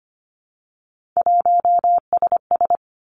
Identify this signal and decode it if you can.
Morse code, heard as a keyed tone, about 25 wpm, 700 Hz. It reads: 1SS
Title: Three Challenging Letter-Number Combinations 25wpm